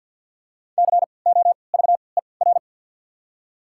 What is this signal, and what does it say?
Morse code, heard as a keyed tone, about 53 wpm, spaced as for 29 wpm, 700 Hz. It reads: COVER
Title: 3x Character Spacing - Top 500 Words - 50wpm